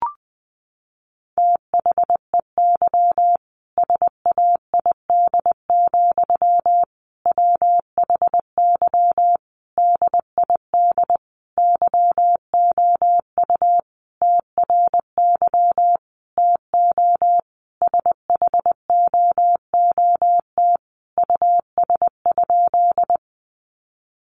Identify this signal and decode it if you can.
Morse code, heard as a keyed tone, about 20 wpm, 700 Hz. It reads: THEY SAID, WHY DID YOU TRY TO SHOOT US?
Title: Sentences from Top 1200 Words 20wpm